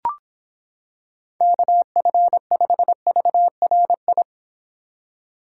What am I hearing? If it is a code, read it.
Morse code, heard as a keyed tone, about 26 wpm, 700 Hz. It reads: KF5VRI